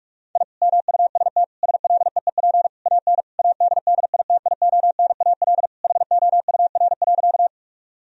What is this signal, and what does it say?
Morse code, heard as a keyed tone, about 41 wpm, 700 Hz. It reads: I MUST SLEEP AN ADDITIONAL HOUR.
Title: Sentences from Top 1000 Words - Review Part 4 - 45wpm